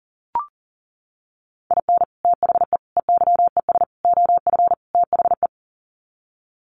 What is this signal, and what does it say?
Morse code, heard as a keyed tone, about 40 wpm, 700 Hz. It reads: IN THE EYES OF THE
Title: Sets of 5 Words 40wpm